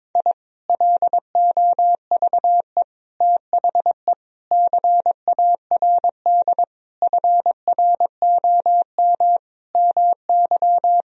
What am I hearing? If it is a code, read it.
Morse code, heard as a keyed tone, about 22 wpm, 700 Hz. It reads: I LOVE THE CARD FROM MY
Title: Sentences from Top 1000 Words - Review Part 4 - 22wpm